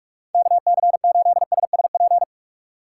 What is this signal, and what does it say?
Morse code, heard as a keyed tone, about 46 wpm, 700 Hz. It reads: KC8SSP
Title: USA Call Signs - 8x Character Spacing - 45wpm